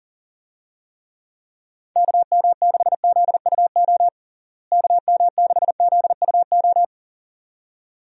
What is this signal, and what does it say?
Morse code, heard as a keyed tone, about 37 wpm, 700 Hz. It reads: KM6ZUO KM6ZUO
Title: USA Call Signs - 6 Characters - Repeated 2x - 40wpm